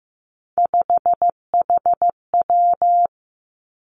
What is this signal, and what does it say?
Morse code, heard as a keyed tone, about 15 wpm, 700 Hz. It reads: 5HW
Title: Three Challenging Letter-Number Combinations 15wpm